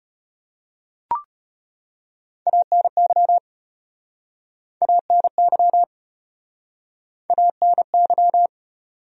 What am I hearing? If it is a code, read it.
Morse code, heard as a keyed tone, about 34 wpm, 700 Hz. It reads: ANY ANY ANY